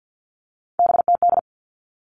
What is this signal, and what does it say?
Morse code, heard as a keyed tone, about 50 wpm, 700 Hz. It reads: BTD
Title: All Cut Numbers - 3-Digit 50wpm